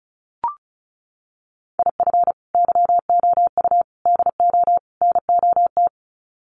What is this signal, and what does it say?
Morse code, heard as a keyed tone, about 34 wpm, 700 Hz. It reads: IF YOU DO NOT
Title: Sets of 4 Words 35wpm